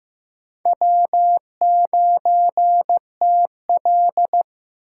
Morse code, heard as a keyed tone, about 15 wpm, 700 Hz. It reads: W9TL
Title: USA Call Signs - 4 Characters - 6x Character Spacing - 15wpm